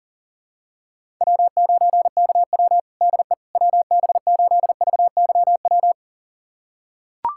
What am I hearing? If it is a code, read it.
Morse code, heard as a keyed tone, about 37 wpm, 700 Hz. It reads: W9KW DE WB8VYW
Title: USA Callsign DE Callsign 40wpm